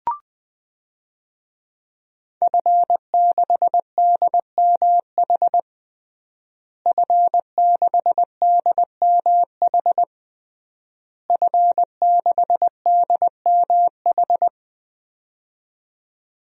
Morse code, heard as a keyed tone, about 20 wpm, 700 Hz. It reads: F6DMH F6DMH F6DMH
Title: DXCC-100 Call Signs - 5 Characters - Repeated 3x - 20wpm